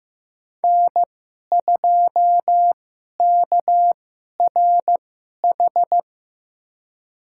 Morse code, heard as a keyed tone, about 16 wpm, spaced as for 11 wpm, 700 Hz. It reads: N2KRH
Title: USA Call Signs - 5 Characters - 2x Character Spacing - 15wpm